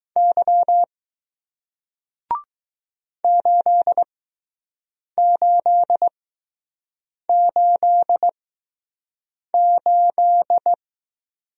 Morse code, heard as a keyed tone, about 20 wpm, spaced as for 6 wpm, 700 Hz. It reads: Y 8888